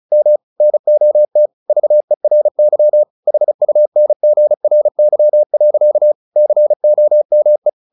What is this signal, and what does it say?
Morse code, heard as a keyed tone, about 34 wpm, 600 Hz. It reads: M NOT VERY HUNGRY. COME
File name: the old man and the sea / the old man and the sea 1 35wpm